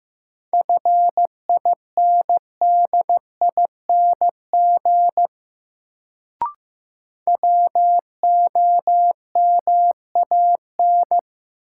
Morse code, heard as a keyed tone, about 15 wpm, 700 Hz. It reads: FINDING WOMAN
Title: Sentences from Top 1100 Words - New Word Drill Alt - 15wpm